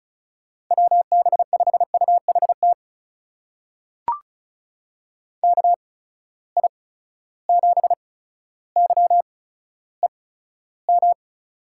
Morse code, heard as a keyed tone, about 36 wpm, spaced as for 15 wpm, 700 Hz. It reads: WB5UHT KI7YEM
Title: USA Call Signs - 6 Characters - 8x Character Spacing - 35wpm